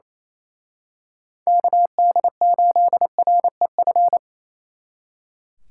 Morse code, heard as a keyed tone, about 28 wpm, 700 Hz. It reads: KD8REF
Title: Speed Racing - USA 2x3 Call Signs 28wpm